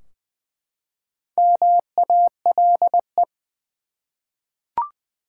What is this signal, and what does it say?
Morse code, heard as a keyed tone, about 20 wpm, 700 Hz. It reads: MALE
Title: Four Letter Words - 20wpm